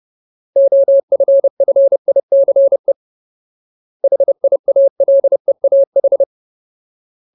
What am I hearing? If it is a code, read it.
Morse code, heard as a keyed tone, about 30 wpm, 550 Hz. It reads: OFFICE HIALEAH
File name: Ultimate Word List / 30WPM Ultimate